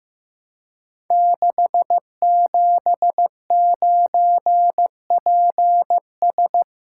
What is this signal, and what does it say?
Morse code, heard as a keyed tone, about 15 wpm, 700 Hz. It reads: 679PS
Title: Code Groups - 5 Letter:Numbers - 15wpm